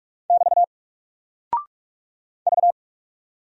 Morse code, heard as a keyed tone, about 46 wpm, 700 Hz. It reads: <BT> V
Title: Single Character (Letter:Number:Punctuation:Prosign) Alt 45wpm